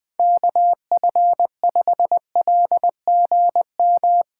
Morse code, heard as a keyed tone, about 19 wpm, 700 Hz. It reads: KF5LGM
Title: Speed Racing - USA 2x3 Call Signs 17wpm